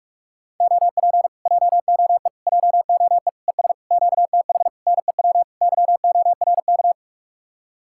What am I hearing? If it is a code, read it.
Morse code, heard as a keyed tone, about 49 wpm, 700 Hz. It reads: OP JOE JOE ES QTH NEW YORK